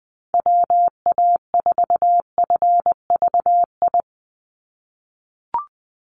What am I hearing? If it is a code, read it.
Morse code, heard as a keyed tone, about 20 wpm, 700 Hz. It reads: WA4FVI
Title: USA Call Signs 20wpm